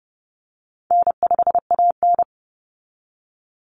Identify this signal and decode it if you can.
Morse code, heard as a keyed tone, about 30 wpm, 700 Hz. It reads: N5AN
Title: Call Signs Repeated 2x - 30wpm